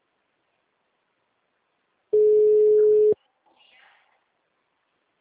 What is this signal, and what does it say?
Ringback tone (Europe, 425 Hz), with speech.